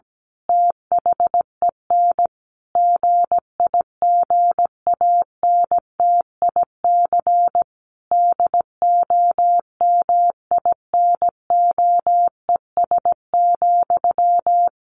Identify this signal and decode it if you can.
Morse code, heard as a keyed tone, about 17 wpm, 700 Hz. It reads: THEN GIGANTIC DOMINOES,
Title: A Sound Of Thunder 17wpm